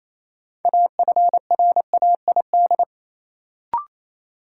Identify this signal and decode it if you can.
Morse code, heard as a keyed tone, about 28 wpm, 700 Hz. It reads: AFRAID
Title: Sentences from Top 1500 Words - New Word Drill - 28wpm